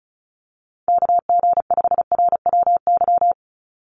Morse code, heard as a keyed tone, about 36 wpm, 700 Hz. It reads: KG5RWY
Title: USA Call Signs 35wpm